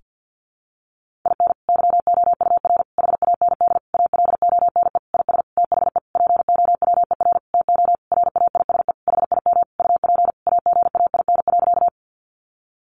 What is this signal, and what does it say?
Morse code, heard as a keyed tone, about 50 wpm, 700 Hz. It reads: IN YOUR HAND ALONE IS THE POWER TO RAISE HIM UP AGAIN.
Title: Sentences from Top 1400 Words 50wpm